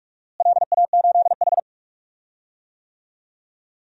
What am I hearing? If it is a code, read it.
Morse code, heard as a keyed tone, about 46 wpm, 700 Hz. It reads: LA8H
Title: DXCC-100 Call Signs - 4 Characters - Repeated 2x - 45wpm